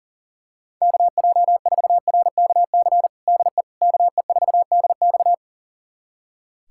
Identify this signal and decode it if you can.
Morse code, heard as a keyed tone, about 40 wpm, 700 Hz. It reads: KJ4RKC DE KE4DX
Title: USA Callsign DE Callsign 40wpm